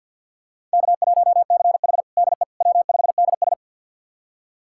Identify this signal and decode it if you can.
Morse code, heard as a keyed tone, about 51 wpm, 700 Hz. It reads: K1CH DE W5DS
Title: USA Callsign DE Callsign - 4 Characters - 50wpm